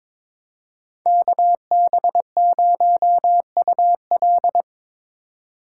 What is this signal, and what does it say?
Morse code, heard as a keyed tone, about 22 wpm, 700 Hz. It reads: KB0UL
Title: Speed Racing - USA 2x2 Call Signs 22wpm